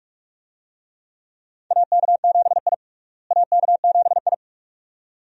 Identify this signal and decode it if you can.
Morse code, heard as a keyed tone, about 46 wpm, 700 Hz. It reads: AK7I AK7I
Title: USA Call Signs - 4 Characters - Repeated 2x - 45wpm